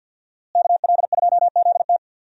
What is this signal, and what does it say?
Morse code, heard as a keyed tone, about 53 wpm, 700 Hz. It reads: KL2ZT
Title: USA Call Signs - 5 Characters - Repeated 2x - 50wpm